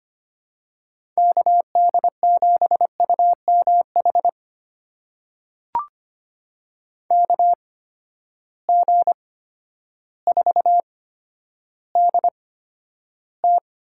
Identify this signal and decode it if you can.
Morse code, heard as a keyed tone, about 25 wpm, spaced as for 11 wpm, 700 Hz. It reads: KD7UMH KG4DT
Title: USA Call Signs - 6 Characters - 8x Character Spacing - 25wpm